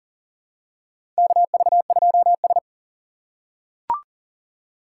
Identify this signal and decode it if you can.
Morse code, heard as a keyed tone, about 40 wpm, 700 Hz. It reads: KV2S
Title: USA Call Signs - 4x Character Spacing - 40wpm